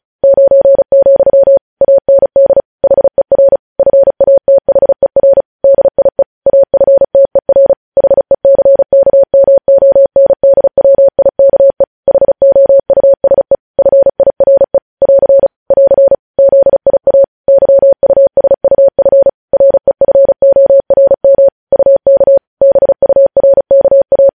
Morse code, heard as a keyed tone, about 36 wpm, 550 Hz. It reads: 9, AND HER FATHER DIE AFTER HECKMONDWIKE HOUSE FIRE <AR> <AR> ZIA YUSUF REFORM UK BURKA
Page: news headlines in morse code at 35 wpm